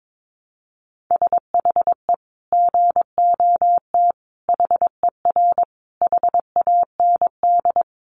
Morse code, heard as a keyed tone, about 22 wpm, 700 Hz. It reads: SHE GOT HER HAND
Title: Sentences from Top 500 Words 22wpm